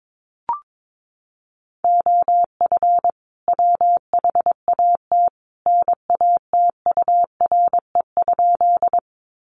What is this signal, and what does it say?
Morse code, heard as a keyed tone, about 22 wpm, 700 Hz. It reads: OF WHAT NATURE?
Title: Sentences from Top 600 Words 22wpm